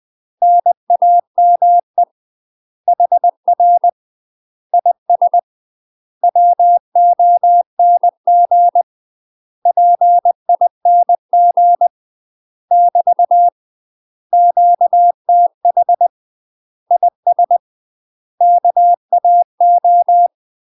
Morse code, heard as a keyed tone, about 20 wpm, 700 Hz. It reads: NAME HR IS WONG PING <BT> QTH IS KAO
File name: UK MASTER QSOs TRAINING / 20WPM / UK MASTER QSOs 0072